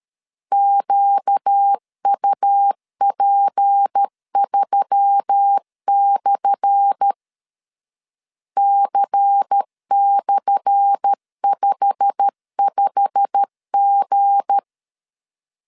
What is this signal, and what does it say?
Morse code, heard as a keyed tone, about 13 wpm, 800 Hz. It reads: QUP3/ C/55G
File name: LEZIONI / 10 SIGNS / 10A SLASH 12wpm 2314